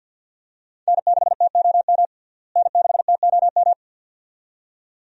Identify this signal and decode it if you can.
Morse code, heard as a keyed tone, about 52 wpm, 700 Hz. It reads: N6TOM N6TOM
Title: USA Call Signs - 5 Characters - Repeated 2x - 50wpm